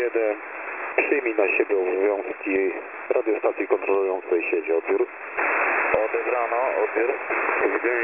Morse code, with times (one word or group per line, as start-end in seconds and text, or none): none